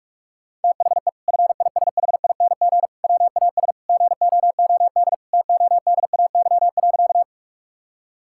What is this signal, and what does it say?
Morse code, heard as a keyed tone, about 44 wpm, 700 Hz. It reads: THE FISHING WAS GOOD TODAY.
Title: Sentences from Top 1000 Words - Review Part 5 - 45wpm